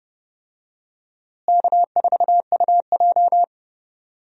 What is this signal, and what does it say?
Morse code, heard as a keyed tone, about 31 wpm, 700 Hz. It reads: K4UJ
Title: USA Call Signs - 4 Characters - Repeated 3x - 30wpm